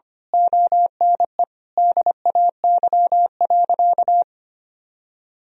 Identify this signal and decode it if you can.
Morse code, heard as a keyed tone, about 25 wpm, 700 Hz. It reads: ONE DAY.
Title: Sentences from Top 1000 Words - Review Part 1 - 25wpm